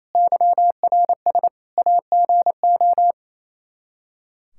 Morse code, heard as a keyed tone, about 27 wpm, 700 Hz. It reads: YRS AGO
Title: Practice QSO - Exchange 3 - More Info - 28wpm